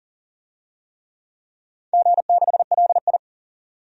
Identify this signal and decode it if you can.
Morse code, heard as a keyed tone, about 37 wpm, 700 Hz. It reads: G6LI
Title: DXCC-100 Call Signs - 4 Characters - 40wpm